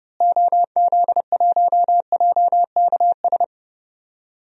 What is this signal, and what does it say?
Morse code, heard as a keyed tone, about 30 wpm, 700 Hz. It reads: OZ1JKS
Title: DXCC-100 Call Signs - 6 Characters - Repeated 2x - 30wpm